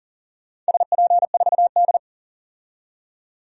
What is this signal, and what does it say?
Morse code, heard as a keyed tone, about 40 wpm, 700 Hz. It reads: SP4D
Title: DXCC-100 Call Signs - 4 Characters - 40wpm